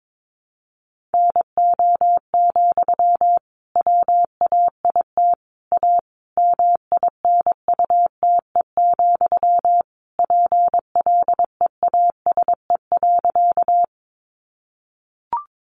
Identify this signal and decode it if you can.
Morse code, heard as a keyed tone, about 22 wpm, 700 Hz. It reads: NO, WAIT A MINUTE, PLEASE.
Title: Sentences from Top 1500 Words 22wpm